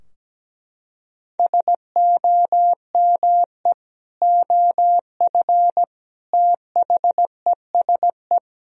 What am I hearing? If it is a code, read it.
Morse code, heard as a keyed tone, about 17 wpm, 700 Hz. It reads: SOME OF THESE
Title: Sentences from Top 1000 Words - Review Part 3 - 17wpm